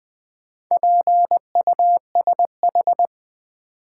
Morse code, heard as a keyed tone, about 20 wpm, 700 Hz. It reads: PUSH
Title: Sentences from Top 1300 Words - New Word Drill - 20wpm